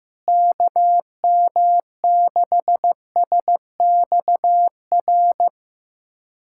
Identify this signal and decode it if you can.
Morse code, heard as a keyed tone, about 16 wpm, 700 Hz. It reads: KM6SXR
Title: USA Callsign DE Callsign - 6 Characters - 15wpm